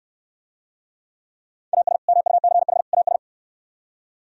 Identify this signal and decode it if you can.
Morse code, heard as a keyed tone, about 65 wpm, 700 Hz. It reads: AS MUCH AS